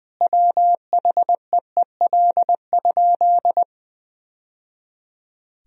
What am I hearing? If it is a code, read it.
Morse code, heard as a keyed tone, about 20 wpm, 700 Hz. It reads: WHEEL?
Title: Sentences from Top 1000 Words - Review Part 3 - 20wpm